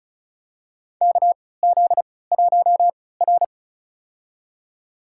Morse code, heard as a keyed tone, about 36 wpm, spaced as for 20 wpm, 700 Hz. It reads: KZ1R
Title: USA Call Signs - 4 Characters - 3x Character Spacing - 35wpm